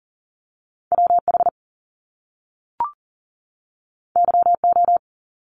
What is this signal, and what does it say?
Morse code, heard as a keyed tone, about 40 wpm, 700 Hz. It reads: WH YO
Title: Top Two Letter Combinations in Words 40wpm